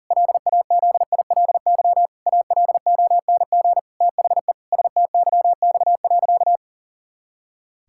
Morse code, heard as a keyed tone, about 40 wpm, 700 Hz. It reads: LAZILY ALONG THE STYX.
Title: Pangram Challenge v2 - 40wpm